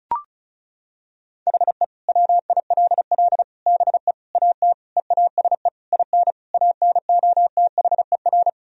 Morse code, heard as a keyed tone, about 36 wpm, 700 Hz. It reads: HE WILL BE AT EASE IN ANOTHER
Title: Sentences from Top 1000 Words - Review Part 1 - 35wpm